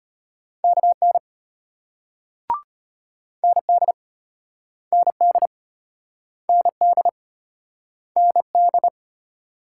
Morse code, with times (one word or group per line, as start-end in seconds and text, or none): 0.6-1.2 KN
3.4-3.9 ND
4.9-5.5 ND
6.5-7.1 ND
8.2-8.9 ND